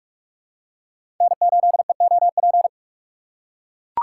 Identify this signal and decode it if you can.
Morse code, heard as a keyed tone, about 43 wpm, 700 Hz. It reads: N8EOP
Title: USA Call Signs - 5 Characters - 3x Character Spacing - 45wpm